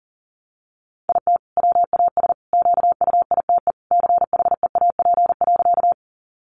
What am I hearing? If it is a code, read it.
Morse code, heard as a keyed tone, about 40 wpm, 700 Hz. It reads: IT WAS QUITE CHEAP.
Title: Sentences from Top 1200 Words 40wpm